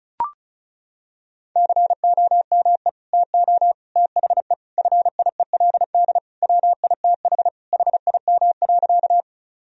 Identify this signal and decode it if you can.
Morse code, heard as a keyed tone, about 36 wpm, 700 Hz. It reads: COME TO THE FIELD WITH HIM.
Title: Sentences from Top 1000 Words - Review Part 2 - 35wpm